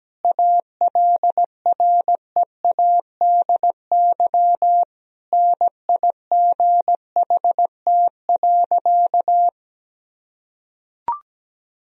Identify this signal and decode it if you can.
Morse code, heard as a keyed tone, about 17 wpm, 700 Hz. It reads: ALREADY NIGHT.
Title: Sentences from Top 1000 Words - Review Part 1 - 17wpm